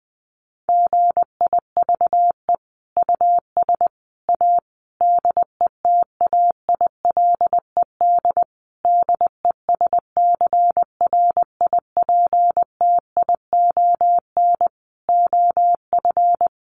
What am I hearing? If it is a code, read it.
Morse code, heard as a keyed tone, about 20 wpm, 700 Hz. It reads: GIVE US A DETAILED DESCRIPTION OF
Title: Sentences from Top 1500 Words 20wpm